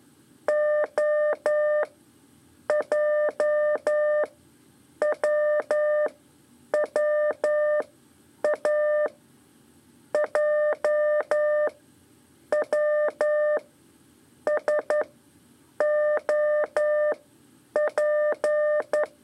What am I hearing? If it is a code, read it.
Morse code, heard as a keyed tone, about 11 wpm, spaced as for 7 wpm, 600 Hz. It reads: OJWWA JWSOP
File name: 3 lezione CW 2 AIR